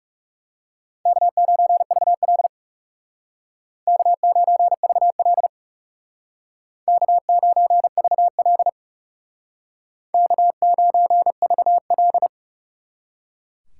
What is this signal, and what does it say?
Morse code, heard as a keyed tone, about 38 wpm, 700 Hz. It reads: K9VL K9VL K9VL K9VL